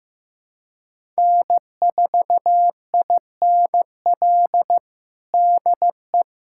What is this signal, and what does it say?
Morse code, heard as a keyed tone, about 15 wpm, 700 Hz. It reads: N4INL DE
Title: USA Callsign DE Callsign - 5 Characters - 15wpm